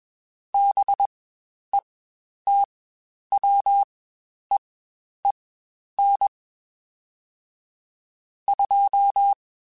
Morse code, heard as a keyed tone, about 21 wpm, spaced as for 10 wpm, 800 Hz. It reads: BETWEEN 2
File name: Spaced Characters / Spaced Characters-mp3s / Weather 4 14 sp @20